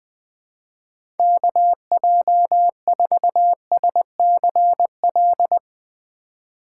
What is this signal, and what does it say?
Morse code, heard as a keyed tone, about 20 wpm, 700 Hz. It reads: KJ4SCL
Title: USA Call Signs - 6 Characters - 3x Character Spacing - 20wpm